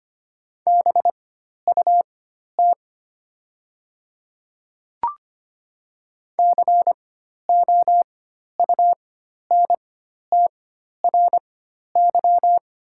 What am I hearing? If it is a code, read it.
Morse code, heard as a keyed tone, about 25 wpm, spaced as for 12 wpm, 700 Hz. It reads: BUT COUNTRY
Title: 4x Character Spacing - Top 300 Words - 25wpm